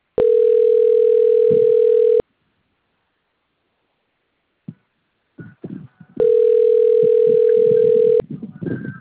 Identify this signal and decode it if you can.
Ringback tone (North America), 440 + 480 Hz.
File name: rg-1600-unknown- - - .47442